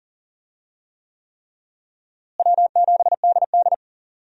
Morse code, heard as a keyed tone, about 40 wpm, 700 Hz. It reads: W7DD
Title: USA Call Signs - 4 Characters - Repeated 3x - 40wpm